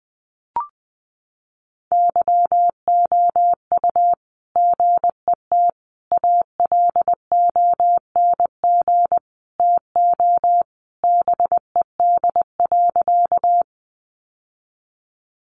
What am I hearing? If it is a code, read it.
Morse code, heard as a keyed tone, about 20 wpm, 700 Hz. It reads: YOU GET ALONG TO BED.
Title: Sentences from Top 700 Words 20wpm